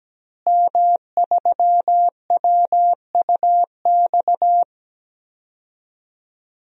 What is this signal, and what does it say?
Morse code, heard as a keyed tone, about 17 wpm, 700 Hz. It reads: M3WUX
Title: DXCC-100 Call Signs - 5 Characters - Repeated 3x - 17wpm